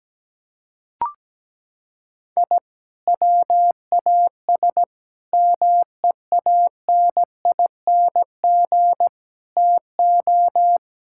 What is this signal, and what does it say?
Morse code, heard as a keyed tone, about 17 wpm, 700 Hz. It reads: I WAS MEANING TO
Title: Sentences from Top 1000 Words - Review Part 6 - 17wpm